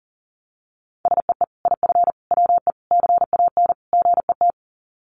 Morse code, heard as a keyed tone, about 40 wpm, 700 Hz. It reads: SEE IF WE CAN GET
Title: Sets of 5 Words Encore2 40wpm